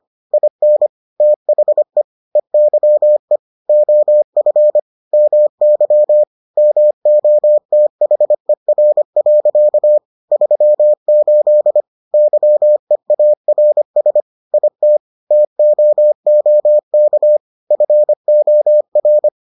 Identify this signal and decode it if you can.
Morse code, heard as a keyed tone, about 25 wpm, 600 Hz. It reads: IN THE EYE OF MY MOTHER. 38 YEARS IT TOOK FOR